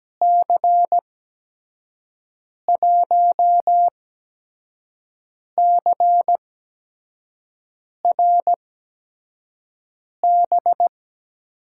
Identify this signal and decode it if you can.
Morse code, heard as a keyed tone, about 17 wpm, spaced as for 5 wpm, 700 Hz. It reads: C1CRB
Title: USA Call Signs - 8x Character Spacing - 17wpm